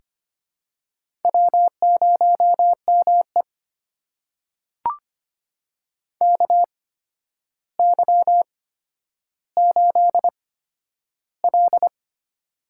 Morse code, heard as a keyed tone, about 25 wpm, spaced as for 11 wpm, 700 Hz. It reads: W0ME KY8L
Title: USA Call Signs - 4 Characters - 8x Character Spacing - 25wpm